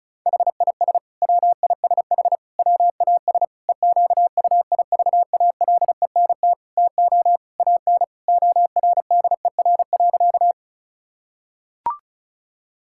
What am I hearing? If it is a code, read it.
Morse code, heard as a keyed tone, about 34 wpm, 700 Hz. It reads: HIS WISH WAS EQUIVALENT TO AN ORDER.
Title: Sentences from Top 1600 Words 35wpm